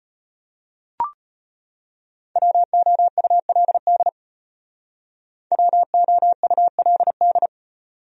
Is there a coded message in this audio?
Yes: Morse code from 2.4 to 7.5 s, heard as a keyed tone, about 33 wpm, 700 Hz. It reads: WOULD WOULD